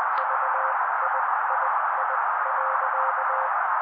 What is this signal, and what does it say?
Morse code, heard as a keyed tone, about 20 wpm, spaced as for 16 wpm, 530 Hz, with noise. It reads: VIII.
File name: daily / 16wpm noise